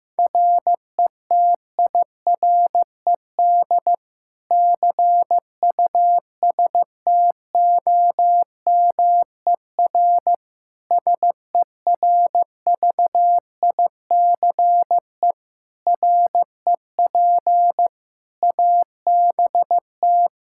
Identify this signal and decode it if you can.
Morse code, heard as a keyed tone, about 15 wpm, 700 Hz. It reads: RETIRED CUSTOMER SERVICE REP ABT
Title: Practice QSO - Exchange 3 - More Info - 15wpm